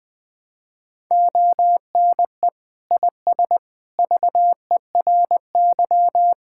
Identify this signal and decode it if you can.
Morse code, heard as a keyed tone, about 21 wpm, 700 Hz. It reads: ONE IS VERY
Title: Sentences from Top 1000 Words - Review Part 5 - 20wpm